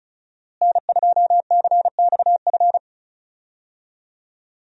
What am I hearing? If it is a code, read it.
Morse code, heard as a keyed tone, about 34 wpm, 700 Hz. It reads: N2CXF
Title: USA Call Signs - 5 Characters - Repeated 3x - 35wpm